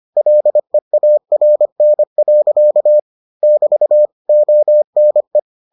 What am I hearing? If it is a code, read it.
Morse code, heard as a keyed tone, about 25 wpm, 600 Hz. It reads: LEARN. <BT> ONE